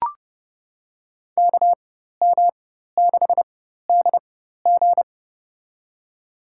Morse code, heard as a keyed tone, about 30 wpm, spaced as for 14 wpm, 700 Hz. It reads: KM6DG
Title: USA Call Signs - 4x Character Spacing - 30wpm